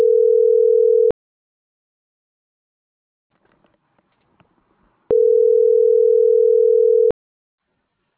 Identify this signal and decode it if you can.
Ringback tone (North America), 440 + 480 Hz.